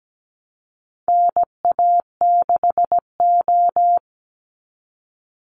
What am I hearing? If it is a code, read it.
Morse code, heard as a keyed tone, about 17 wpm, 700 Hz. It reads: NA6O
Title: NA SOTA Chaser Callsigns 17wpm